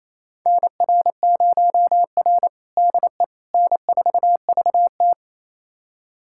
Morse code, heard as a keyed tone, about 28 wpm, 700 Hz. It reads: NR0R DE N4VT
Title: USA Callsign DE Callsign - 4 Characters - 28wpm